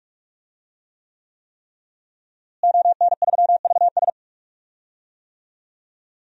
Morse code, heard as a keyed tone, about 46 wpm, 700 Hz. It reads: ON3VS
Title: DXCC-100 Call Signs - 5 Characters - Repeated 3x - 45wpm